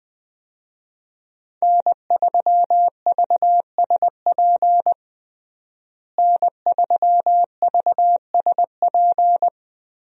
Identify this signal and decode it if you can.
Morse code, heard as a keyed tone, about 20 wpm, 700 Hz. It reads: N3VSP N3VSP
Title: USA Call Signs - 5 Characters - Repeated 2x - 20wpm